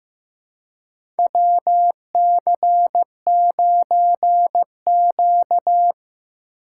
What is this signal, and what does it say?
Morse code, heard as a keyed tone, about 16 wpm, 700 Hz. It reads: WC9Q
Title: USA Call Signs - 4x Character Spacing - 15wpm